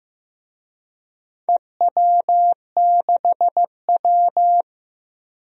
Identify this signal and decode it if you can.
Morse code, heard as a keyed tone, about 15 wpm, 700 Hz. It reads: EW6W
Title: DXCC-100 Call Signs - 4 Characters - Repeated 2x - 15wpm